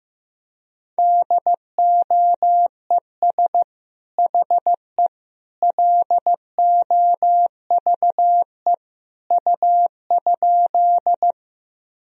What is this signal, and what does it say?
Morse code, heard as a keyed tone, about 15 wpm, 700 Hz. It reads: DOES HE LOVE U?
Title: Sentences from Top 400 Words and Abbreviations 15wpm